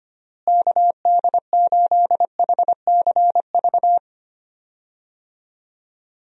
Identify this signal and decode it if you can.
Morse code, heard as a keyed tone, about 25 wpm, 700 Hz. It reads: KD8HCV
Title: USA Call Signs - 6 Characters - Repeated 2x - 25wpm